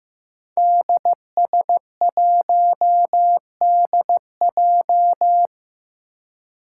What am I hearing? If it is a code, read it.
Morse code, heard as a keyed tone, about 15 wpm, 700 Hz. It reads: DS1DJ
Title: Five Challenging Letter-Number Combinations 15wpm